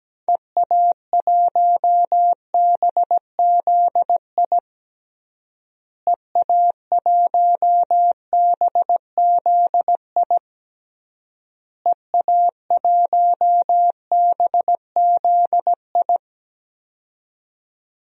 Morse code, heard as a keyed tone, about 17 wpm, 700 Hz. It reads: EA1BZI EA1BZI EA1BZI
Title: DXCC-100 Call Signs - 6 Characters - Repeated 3x - 17wpm